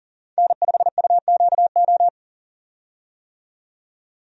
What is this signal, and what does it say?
Morse code, heard as a keyed tone, about 40 wpm, 700 Hz. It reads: N5UQO
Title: USA Call Signs - 5 Characters - 40wpm